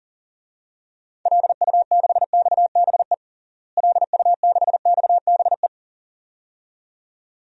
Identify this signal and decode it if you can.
Morse code, heard as a keyed tone, about 40 wpm, 700 Hz. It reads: LU6XBE LU6XBE
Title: DXCC-100 Call Signs - 6 Characters - Repeated 2x - 40wpm